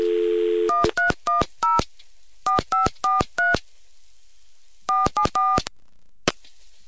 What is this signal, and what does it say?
DTMF, heard as telephone touch tones, tones of about 160 ms. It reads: 1310 4543 474